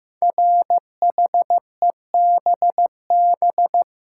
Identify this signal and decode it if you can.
Morse code, heard as a keyed tone, about 15 wpm, 700 Hz. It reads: RHEBB
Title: Code Groups - 5 Letters - 15wpm